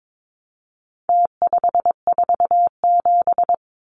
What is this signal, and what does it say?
Morse code, heard as a keyed tone, about 22 wpm, 700 Hz. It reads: T547
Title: Numbers - Contest Style - 4-Digit 22wpm